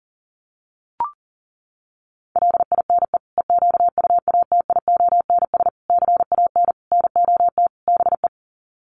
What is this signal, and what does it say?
Morse code, heard as a keyed tone, about 40 wpm, 700 Hz. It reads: LINE EQUATIONS CAN NOT BE
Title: Sets of 5 Words 40wpm